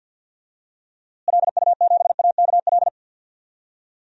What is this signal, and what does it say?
Morse code, heard as a keyed tone, about 52 wpm, 700 Hz. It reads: LU7AKL
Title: DXCC-100 Call Signs - 6 Characters - Repeated 3x - 50wpm